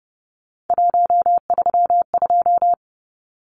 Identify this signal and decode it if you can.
Morse code, heard as a keyed tone, about 30 wpm, 700 Hz. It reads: 132
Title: Numbers - 3-Digit 30wpm